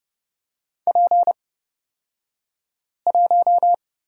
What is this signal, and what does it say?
Morse code, heard as a keyed tone, about 30 wpm, spaced as for 5 wpm, 700 Hz. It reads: P1
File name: Lesson 37d - Letter x - Two Characters 30wpm